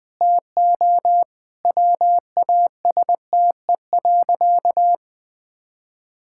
Morse code, heard as a keyed tone, about 20 wpm, 700 Hz. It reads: TO WASTE.
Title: Sentences with 3-Word Phrases - Spoken Beginning - 20wpm